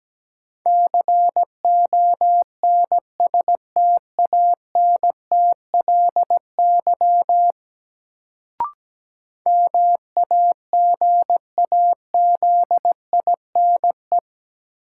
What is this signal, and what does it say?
Morse code, heard as a keyed tone, about 17 wpm, 700 Hz. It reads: CONSTANTLY MAGAZINE
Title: Sentences from Top 1000 Words - Review Part 7 - New Word Drill - 17wpm